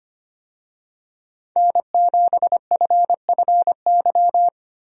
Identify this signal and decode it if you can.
Morse code, heard as a keyed tone, about 25 wpm, 700 Hz. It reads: N7FFY
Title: USA Call Signs - 5 Characters - 25wpm